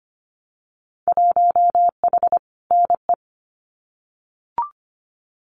Morse code, heard as a keyed tone, about 25 wpm, 700 Hz. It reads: 1H NE
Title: Winter Field Day Exchanges Encore 25wpm